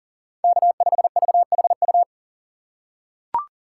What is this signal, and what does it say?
Morse code, heard as a keyed tone, about 40 wpm, 700 Hz. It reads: K5VHU
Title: USA Call Signs - 5 Characters - 2x Character Spacing - 40wpm